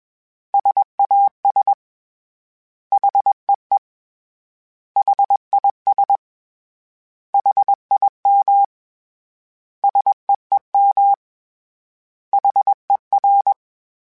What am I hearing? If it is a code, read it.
Morse code, heard as a keyed tone, about 21 wpm, 800 Hz. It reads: SAS HEE HIS HIM SEEM HER
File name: S vs H / S vs H mp3s / S vs H @20